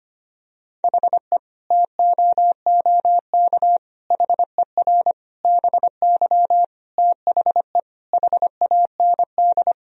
Morse code, heard as a keyed tone, about 25 wpm, 700 Hz. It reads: HE TOOK HER BY THE HAND
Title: Sentences from Top 1000 Words - Review Part 2 - 25wpm